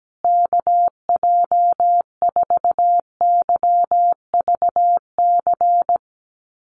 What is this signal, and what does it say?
Morse code, heard as a keyed tone, about 17 wpm, 700 Hz. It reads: KJ4YVC
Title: USA Call Signs Repeated 2x 17wpm